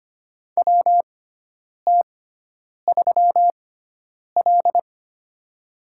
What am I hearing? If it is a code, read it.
Morse code, heard as a keyed tone, about 25 wpm, spaced as for 9 wpm, 700 Hz. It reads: WT3L
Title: USA Call Signs - 6x Character Spacing - 25wpm